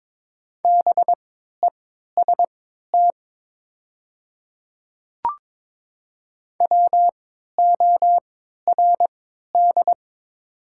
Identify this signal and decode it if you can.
Morse code, heard as a keyed tone, about 23 wpm, spaced as for 13 wpm, 700 Hz. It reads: BEST WORD
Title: 3x Character Spacing - Top 500 Words - 22wpm